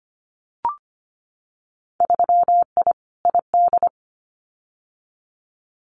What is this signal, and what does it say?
Morse code, heard as a keyed tone, about 25 wpm, 700 Hz. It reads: 3I ID
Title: Winter Field Day Exchanges 25wpm